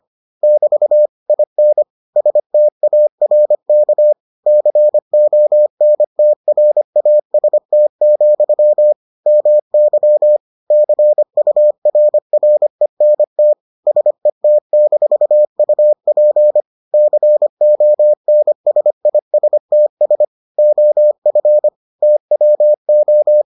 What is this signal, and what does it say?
Morse code, heard as a keyed tone, about 25 wpm, 600 Hz. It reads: <BT> IN STARK CONTRAST, MY CURRENT SET-UP CONSISTS OF TWO